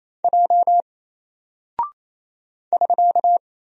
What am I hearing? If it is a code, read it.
Morse code, heard as a keyed tone, about 28 wpm, 700 Hz. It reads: J <SK>
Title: Single Character (Letter:Number:Punctuation:Prosign) Alt 28wpm